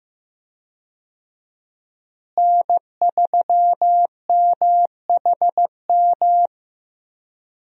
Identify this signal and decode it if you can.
Morse code, heard as a keyed tone, about 15 wpm, 700 Hz. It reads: N3MHM
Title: USA Call Signs - 5 Characters - Repeated 3x - 15wpm